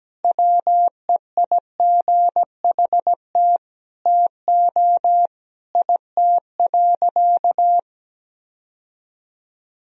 Morse code, heard as a keyed tone, about 17 wpm, 700 Hz. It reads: WEIGHT TO IT.
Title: Sentences from Top 1000 Words - Review Part 3 - 17wpm